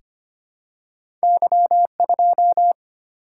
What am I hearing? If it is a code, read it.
Morse code, heard as a keyed tone, about 25 wpm, 700 Hz. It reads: Y2
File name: Two Letter-Number Combinations - 25wpm